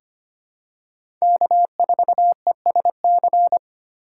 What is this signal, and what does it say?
Morse code, heard as a keyed tone, about 26 wpm, 700 Hz. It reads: K4ESC
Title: USA Call Signs - 5 Characters - 8x Character Spacing - 25wpm